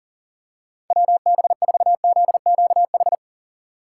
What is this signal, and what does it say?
Morse code, heard as a keyed tone, about 40 wpm, 700 Hz. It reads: WB4ZQH
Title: USA Call Signs - 4x Character Spacing - 40wpm